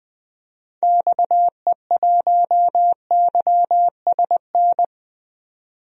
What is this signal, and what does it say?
Morse code, heard as a keyed tone, about 20 wpm, 700 Hz. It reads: XE1YSN
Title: DXCC-100 Call Signs - 6 Characters - 20wpm